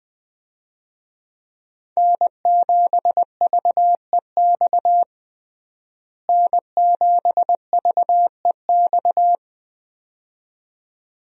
Morse code, heard as a keyed tone, about 21 wpm, 700 Hz. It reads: N7VEX N7VEX
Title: USA Call Signs - 5 Characters - Repeated 2x - 20wpm